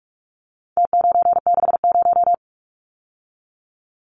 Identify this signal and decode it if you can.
Morse code, heard as a keyed tone, about 44 wpm, 700 Hz. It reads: T960
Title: Numbers - Contest Style - 4-Digit 45wpm